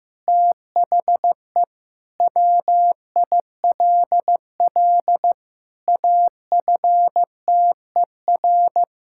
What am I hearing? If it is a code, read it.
Morse code, heard as a keyed tone, about 15 wpm, 700 Hz. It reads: THE WILL AFTER
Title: Sentences from Top 1600 Words 15wpm